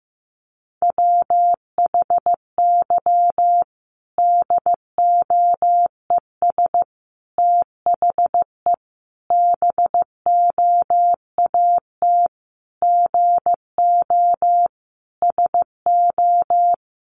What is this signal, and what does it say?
Morse code, heard as a keyed tone, about 15 wpm, 700 Hz. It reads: WHY DOES THE BOAT GO SO
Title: Sentences from Top 500 Words 15wpm